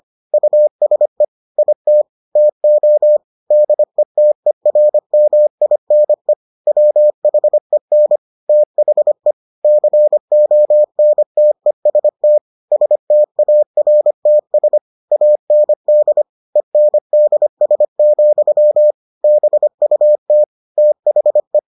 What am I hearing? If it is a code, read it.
Morse code, heard as a keyed tone, about 25 wpm, 600 Hz. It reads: USE IT TO DETERMINE WHEN THE CONTEST STARTS AND ENDS, BUT THE